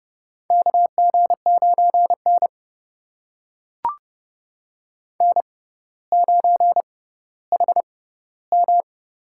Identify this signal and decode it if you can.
Morse code, heard as a keyed tone, about 31 wpm, spaced as for 16 wpm, 700 Hz. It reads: KG9N N9HM
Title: USA Call Signs - 6x Character Spacing - 30wpm